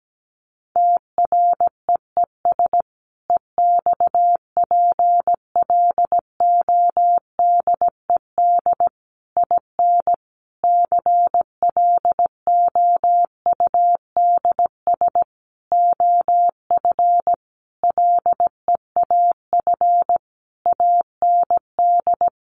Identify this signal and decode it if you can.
Morse code, heard as a keyed tone, about 17 wpm, 700 Hz. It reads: TREES EXPLODED IN CLOUDS OF LEAF AND
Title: A Sound Of Thunder 17wpm